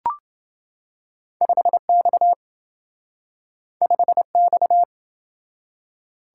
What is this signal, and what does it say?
Morse code, heard as a keyed tone, about 27 wpm, 700 Hz. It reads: 5X 5X